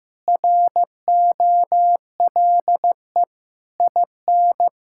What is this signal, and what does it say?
Morse code, heard as a keyed tone, about 15 wpm, 700 Hz. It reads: ROLE IN
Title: Sentences from Top 1000 Words - Review Part 6 - 15wpm